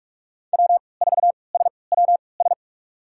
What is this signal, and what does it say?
Morse code, heard as a keyed tone, about 47 wpm, spaced as for 26 wpm, 700 Hz. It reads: W4SWS
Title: USA Call Signs - 5 Characters - 3x Character Spacing - 45wpm